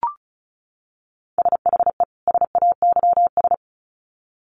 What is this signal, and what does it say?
Morse code, heard as a keyed tone, about 34 wpm, 700 Hz. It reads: SHE SAYS
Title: Sets of 2 Words 35wpm